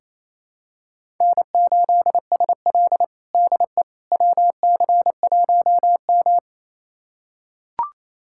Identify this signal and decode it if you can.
Morse code, heard as a keyed tone, about 28 wpm, 700 Hz. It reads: N8SL DE WC1M
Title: USA Callsign DE Callsign - 4 Characters - 28wpm